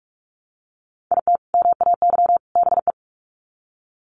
Morse code, heard as a keyed tone, about 44 wpm, 700 Hz. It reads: IT MAY BE
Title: Sets of 3 Words 45wpm